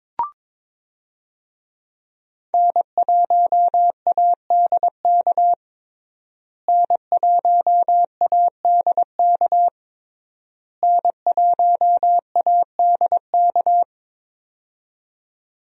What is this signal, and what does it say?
Morse code, heard as a keyed tone, about 22 wpm, 700 Hz. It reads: N1ADK N1ADK N1ADK
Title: USA Call Signs - 5 Characters - Repeated 3x - 22wpm